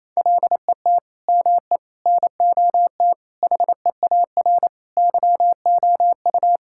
Morse code, heard as a keyed tone, about 28 wpm, 700 Hz. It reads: LET ME NOT HEAR YOU
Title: Sentences from Top 1600 Words 28wpm